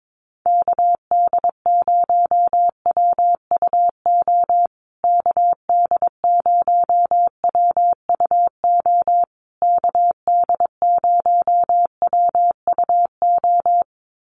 Morse code, heard as a keyed tone, about 22 wpm, 700 Hz. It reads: KD0WUO KD0WUO KD0WUO
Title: USA Call Signs Repeated 3x 22wpm